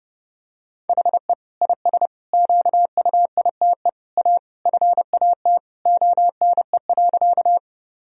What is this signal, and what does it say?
Morse code, heard as a keyed tone, about 32 wpm, 700 Hz. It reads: HE IS QUITE A FAT ONE.
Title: Sentences from Top 1000 Words - Review Part 4 - 30wpm